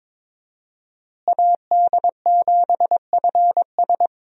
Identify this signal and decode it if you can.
Morse code, heard as a keyed tone, about 22 wpm, 700 Hz. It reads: AD7FS
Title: USA Call Signs - 5 Characters - Repeated 2x - 22wpm